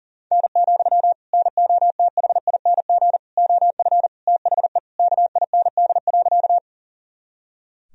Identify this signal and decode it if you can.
Morse code, heard as a keyed tone, about 40 wpm, 700 Hz. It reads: N, NOTHING OF THE KIND.
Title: Sentences from Top 500 Words and Abbreviations 40wpm